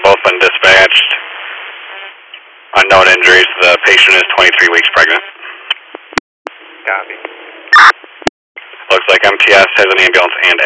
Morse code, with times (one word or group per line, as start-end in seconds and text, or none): none